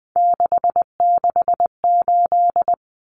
Morse code, heard as a keyed tone, about 20 wpm, 700 Hz. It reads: 668
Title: Numbers - 3-Digit 20wpm